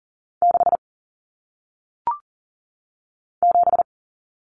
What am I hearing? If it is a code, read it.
Morse code, heard as a keyed tone, about 40 wpm, 700 Hz. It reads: 6 7
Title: Numbers - 1-Digit 40wpm